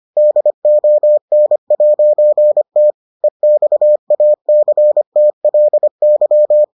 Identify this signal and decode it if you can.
Morse code, heard as a keyed tone, about 25 wpm, 600 Hz. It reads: DON'T EXACTLY